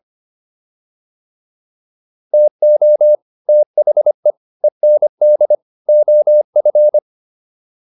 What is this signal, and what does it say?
Morse code, heard as a keyed tone, about 25 wpm, 600 Hz. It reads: TO THE END OF